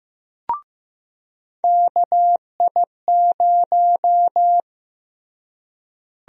Morse code, heard as a keyed tone, about 15 wpm, 700 Hz. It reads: KI0
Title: USA Call Sign Prefixes - Repeated 2x - 15wpm